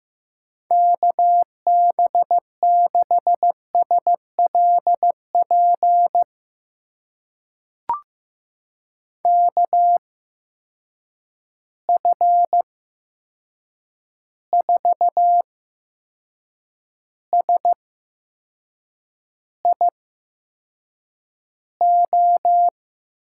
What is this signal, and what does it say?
Morse code, heard as a keyed tone, about 15 wpm, spaced as for 6 wpm, 700 Hz. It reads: KB6SLP KF4SIO
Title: USA Call Signs - 6 Characters - 8x Character Spacing - 15wpm